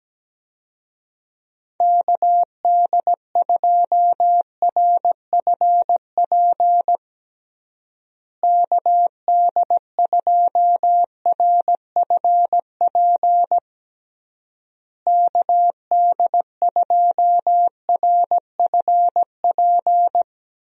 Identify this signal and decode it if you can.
Morse code, heard as a keyed tone, about 17 wpm, 700 Hz. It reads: KD2RFP KD2RFP KD2RFP
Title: USA Call Signs - 6 Characters - Repeated 3x - 17wpm